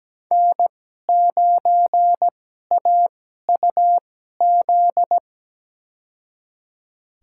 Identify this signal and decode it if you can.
Morse code, heard as a keyed tone, about 17 wpm, spaced as for 12 wpm, 700 Hz. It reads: N9AUZ
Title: USA Call Signs - 5 Characters - 2x Character Spacing - 17wpm